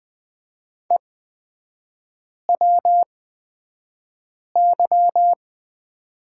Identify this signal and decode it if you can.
Morse code, heard as a keyed tone, about 20 wpm, spaced as for 5 wpm, 700 Hz. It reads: EWY
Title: 250 Lesson 40e - BK - Three Characters 20wpm